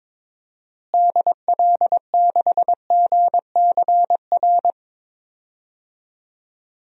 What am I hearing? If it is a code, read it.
Morse code, heard as a keyed tone, about 22 wpm, 700 Hz. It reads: DL6GCR
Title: DXCC-100 Call Signs - 6 Characters - Repeated 3x - 22wpm